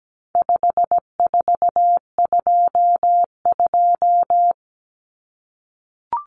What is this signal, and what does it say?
Morse code, heard as a keyed tone, about 17 wpm, 700 Hz. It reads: 5422
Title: Numbers - 4-Digit 17wpm